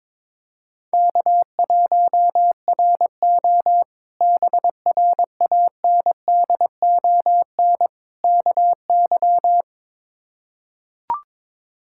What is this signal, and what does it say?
Morse code, heard as a keyed tone, about 22 wpm, 700 Hz. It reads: K1RO BRANDON KY
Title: K1USN SST QSO Practice - 22wpm